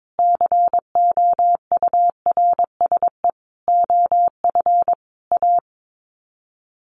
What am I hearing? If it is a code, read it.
Morse code, heard as a keyed tone, about 22 wpm, 700 Hz. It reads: COURSE OF A
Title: Sets of 5 Words Encore 22wpm